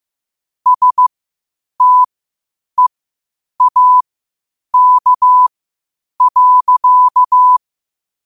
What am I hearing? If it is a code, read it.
Morse code, heard as a keyed tone, about 15 wpm, spaced as for 8 wpm, 1000 Hz. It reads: STEAK.
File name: qotdfiles / 8wpm